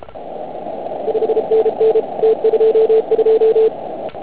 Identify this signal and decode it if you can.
Morse code, heard as a keyed tone, about 33 wpm, 440 Hz, with noise. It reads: 5NN T22
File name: Archiv / CUC149 / OK1IF CUC149